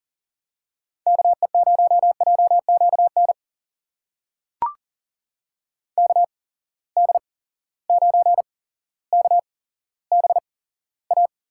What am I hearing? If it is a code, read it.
Morse code, heard as a keyed tone, about 40 wpm, spaced as for 17 wpm, 700 Hz. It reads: KE0JQN KD9KBA
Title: USA Call Signs - 6 Characters - 8x Character Spacing - 40wpm